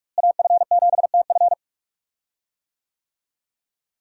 Morse code, heard as a keyed tone, about 46 wpm, 700 Hz. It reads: AF7TF
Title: USA Call Signs - 5 Characters - Repeated 2x - 45wpm